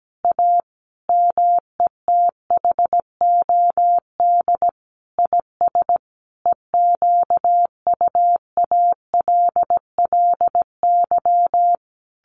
Morse code, heard as a keyed tone, about 17 wpm, 700 Hz. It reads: A METHOD IS EQUALLY
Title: Sentences from Top 1400 Words 17wpm